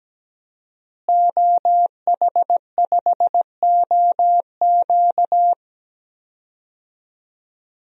Morse code, heard as a keyed tone, about 17 wpm, 700 Hz. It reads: OH5OQ
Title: DXCC-100 Call Signs - 5 Characters - Repeated 3x - 17wpm